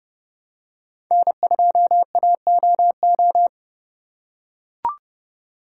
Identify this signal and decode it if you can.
Morse code, heard as a keyed tone, about 30 wpm, 700 Hz. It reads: N2AOO
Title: USA Call Signs - 5 Characters - 3x Character Spacing - 30wpm